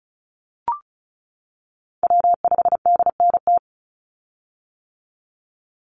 Morse code, heard as a keyed tone, about 36 wpm, 700 Hz. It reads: W5DNT
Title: Call Signs - 35wpm